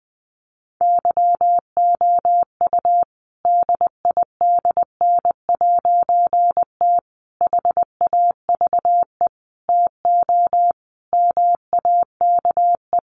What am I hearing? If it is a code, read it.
Morse code, heard as a keyed tone, about 20 wpm, 700 Hz. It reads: YOU DIDN'T HAVE TO MAKE
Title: A Sound Of Thunder 20wpm